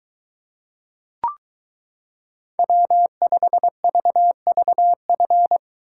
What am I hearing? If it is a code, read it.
Morse code, heard as a keyed tone, about 22 wpm, 700 Hz. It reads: W5VVF